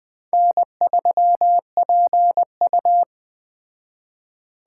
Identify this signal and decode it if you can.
Morse code, heard as a keyed tone, about 20 wpm, 700 Hz. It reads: N3PU
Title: Speed Racing - USA 1x2 Call Signs 17wpm